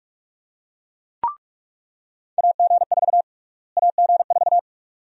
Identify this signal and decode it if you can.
Morse code, heard as a keyed tone, about 42 wpm, 700 Hz. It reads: AG4 AG4
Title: USA Call Sign Prefixes - Repeated 2x - 45wpm